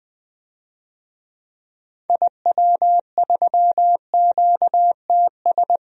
Morse code, heard as a keyed tone, about 21 wpm, 700 Hz. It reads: IW3QTS
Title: DXCC-100 Call Signs - 6 Characters - 20wpm